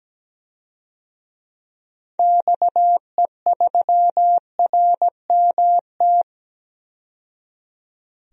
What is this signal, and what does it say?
Morse code, heard as a keyed tone, about 17 wpm, 700 Hz. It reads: XE3RMT
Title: DXCC-100 Call Signs - 6 Characters - 17wpm